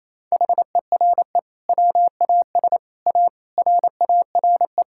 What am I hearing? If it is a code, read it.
Morse code, heard as a keyed tone, about 28 wpm, 700 Hz. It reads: HERE WAS A RARE
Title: Sentences from Top 1600 Words 28wpm